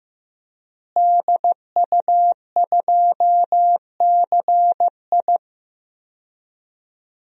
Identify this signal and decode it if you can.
Morse code, heard as a keyed tone, about 15 wpm, 700 Hz. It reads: DU2CI
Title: DXCC-100 Call Signs - 5 Characters - Repeated 3x - 15wpm